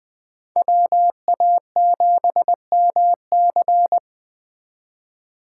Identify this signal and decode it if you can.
Morse code, heard as a keyed tone, about 21 wpm, 700 Hz. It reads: WA7MC
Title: Speed Racing - USA 2x2 Call Signs 17wpm